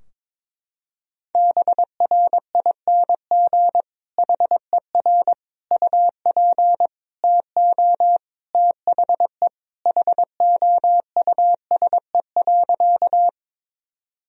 Morse code, heard as a keyed tone, about 22 wpm, 700 Hz. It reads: BRING HER UP TO THE HOUSE.
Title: Sentences from Top 1000 Words - Review Part 3 - 22wpm